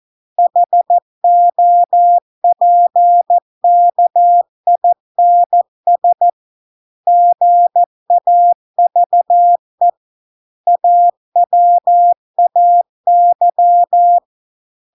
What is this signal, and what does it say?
Morse code, heard as a keyed tone, about 14 wpm, 700 Hz. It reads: HOPKINS GAVE AWAY